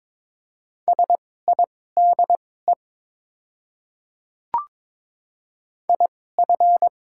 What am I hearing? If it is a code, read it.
Morse code, heard as a keyed tone, about 22 wpm, spaced as for 16 wpm, 700 Hz. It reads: SIDE IF
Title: 2x Character Spacing - Top 100 Words - 22wpm